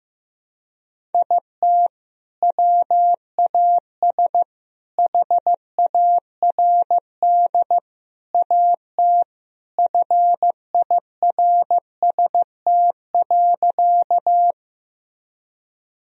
Morse code, heard as a keyed tone, about 16 wpm, 700 Hz. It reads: IT WAS HARD AT FIRST.
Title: Sentences from Top 1000 Words - Review Part 1 - 15wpm